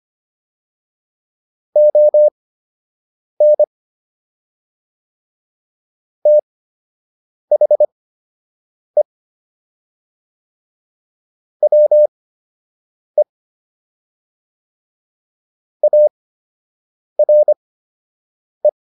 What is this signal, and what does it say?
Morse code, heard as a keyed tone, about 25 wpm, spaced as for 7 wpm, 600 Hz. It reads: ON THE WE ARE